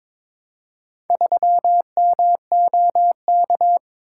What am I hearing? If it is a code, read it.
Morse code, heard as a keyed tone, about 22 wpm, 700 Hz. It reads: 3MOK
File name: Four Letter-Number Combinations - 22wpm